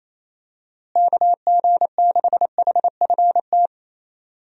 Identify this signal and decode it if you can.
Morse code, heard as a keyed tone, about 28 wpm, 700 Hz. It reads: KG6HFT
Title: USA Call Signs - 6 Characters - Repeated 3x - 28wpm